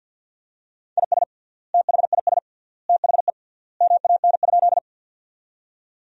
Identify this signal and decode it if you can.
Morse code, heard as a keyed tone, about 51 wpm, 700 Hz. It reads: IS THIS THE MAN?
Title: 3x Word-Spacing - Sentences from Top 200 Words - 50wpm